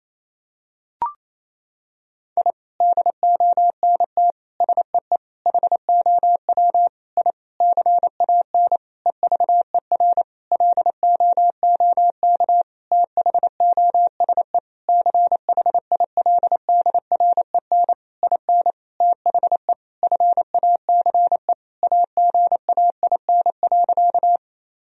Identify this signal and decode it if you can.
Morse code, heard as a keyed tone, about 28 wpm, 700 Hz. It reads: I DONT SEE HOW I CAN EVER LOOK THOSE CHILDREN IN THE FACE AGAIN.
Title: Sentences from Top 1000 Words - Review Part 2 - 28wpm